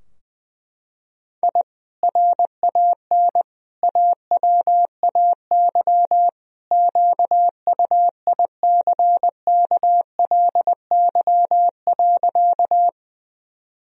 Morse code, heard as a keyed tone, about 20 wpm, 700 Hz. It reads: I RAN AWAY QUICKLY.
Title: Sentences from Top 1000 Words - Review Part 3 - 20wpm